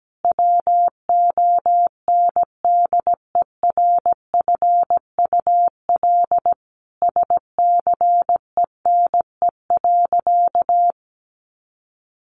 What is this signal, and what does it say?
Morse code, heard as a keyed tone, about 17 wpm, 700 Hz. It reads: WONDERFUL SCENE.
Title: Sentences from Top 1100 Words 17wpm